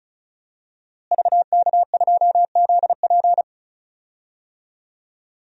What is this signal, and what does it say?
Morse code, heard as a keyed tone, about 34 wpm, 700 Hz. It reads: VK2ZP
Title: DXCC-100 Call Signs - 5 Characters - Repeated 2x - 35wpm